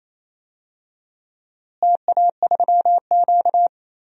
Morse code, heard as a keyed tone, about 28 wpm, 700 Hz. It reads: TA3Q
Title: DXCC-100 Call Signs - 4 Characters - Repeated 3x - 28wpm